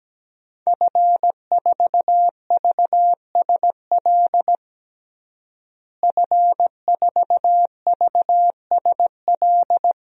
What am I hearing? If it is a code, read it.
Morse code, heard as a keyed tone, about 17 wpm, 700 Hz. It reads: F4VSL F4VSL
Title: DXCC-100 Call Signs - 5 Characters - Repeated 2x - 17wpm